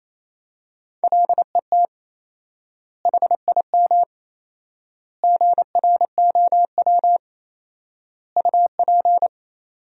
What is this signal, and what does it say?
Morse code, heard as a keyed tone, about 28 wpm, 700 Hz. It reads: LET HIM GROW UP
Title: 4x Word-Spacing - Sentences from Top 300 Words - 28wpm